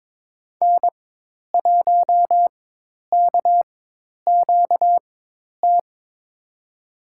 Morse code, heard as a keyed tone, about 22 wpm, spaced as for 10 wpm, 700 Hz. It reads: N1KQT
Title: USA Call Signs - 4x Character Spacing - 22wpm